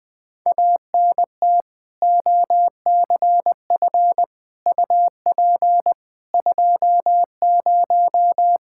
Morse code, heard as a keyed tone, about 21 wpm, 700 Hz. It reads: ANT OCF UP 20
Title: K1USN SST QSO Practice 20wpm